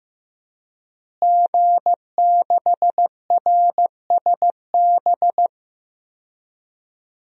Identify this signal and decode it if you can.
Morse code, heard as a keyed tone, about 15 wpm, 700 Hz. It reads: G6RSB
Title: DXCC-100 Call Signs - 5 Characters - Repeated 3x - 15wpm